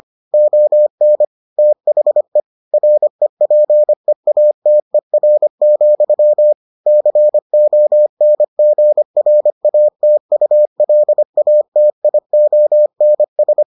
Morse code, heard as a keyed tone, about 25 wpm, 600 Hz. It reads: ON THE REPEATER, CONGRATULATIONS